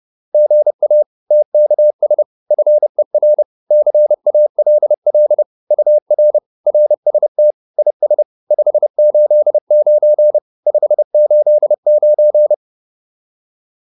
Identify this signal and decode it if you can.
Morse code, heard as a keyed tone, about 30 wpm, 600 Hz. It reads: GA TKS FER CALL UR RST IS 589 589